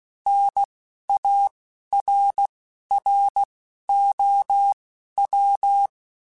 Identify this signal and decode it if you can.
Morse code, heard as a keyed tone, about 16 wpm, spaced as for 12 wpm, 800 Hz. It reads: NARROW
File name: Spaced / Chuckles Spaced-mp3s / MiddleAge sp @15